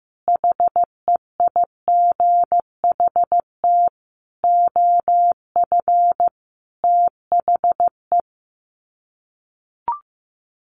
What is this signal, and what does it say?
Morse code, heard as a keyed tone, about 15 wpm, 700 Hz. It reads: HEIGHT OF THE
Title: Sets of 5 Words 15wpm